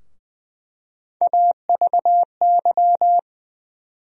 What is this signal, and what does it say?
Morse code, heard as a keyed tone, about 20 wpm, 700 Hz. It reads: AVY
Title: USA Call Sign Postfixes - 3 Letters 20wpm